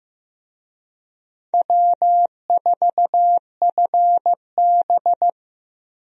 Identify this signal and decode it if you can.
Morse code, heard as a keyed tone, about 15 wpm, 700 Hz. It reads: W4FB
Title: USA Call Signs - 4 Characters - 15wpm